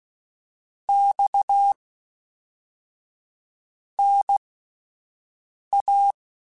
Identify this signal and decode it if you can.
Morse code, heard as a keyed tone, about 16 wpm, spaced as for 4 wpm, 800 Hz. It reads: XNA
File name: CW Comparisons / CW Comparisons-mp3s / CW Comparisons Broken Characters @10